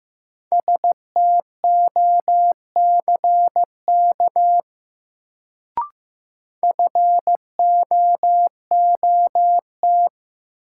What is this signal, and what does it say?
Morse code, heard as a keyed tone, about 15 wpm, 700 Hz. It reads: STOCK FOOT
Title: Sets of 5 Words - New Word Drill - 15wpm